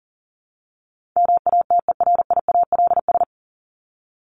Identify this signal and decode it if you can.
Morse code, heard as a keyed tone, about 40 wpm, 700 Hz. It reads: MATERIALS
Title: ES:S Suffix - Top 100 Words with Base Words 40wpm